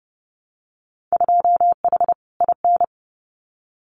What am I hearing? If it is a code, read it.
Morse code, heard as a keyed tone, about 30 wpm, 700 Hz. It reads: 2H IN
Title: Winter Field Day Exchanges Encore 30wpm